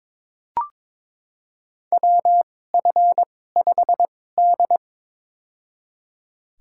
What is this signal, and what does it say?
Morse code, heard as a keyed tone, about 22 wpm, spaced as for 16 wpm, 700 Hz. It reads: WF5D
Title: USA Call Signs - 2x Character Spacing - 22wpm